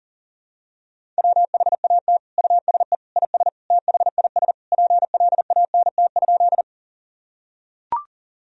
Morse code, heard as a keyed tone, about 40 wpm, 700 Hz. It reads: WHAT USE IS THIS PLANT?
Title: Sentences from Top 1000 Words - Review Part 1 - 40wpm